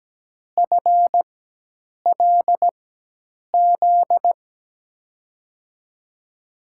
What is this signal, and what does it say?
Morse code, heard as a keyed tone, about 17 wpm, spaced as for 8 wpm, 700 Hz. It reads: FLZ
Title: USA Call Signs - 6 Characters - 4x Character Spacing - 17wpm